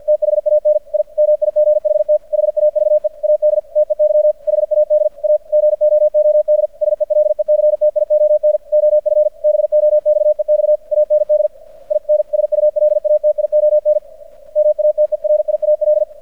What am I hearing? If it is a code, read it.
Morse code, heard as a keyed tone, about 50 wpm, 600 Hz. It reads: THAN I MIGHT HAVE AD TEY HAD A GOOD SELECTION OF BOXEX AND INSULATION MATERIAL
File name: archived / CRSnet